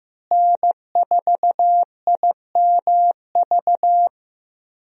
Morse code, heard as a keyed tone, about 15 wpm, 700 Hz. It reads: N4IMV
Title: USA Call Signs - 5 Characters - Repeated 2x - 15wpm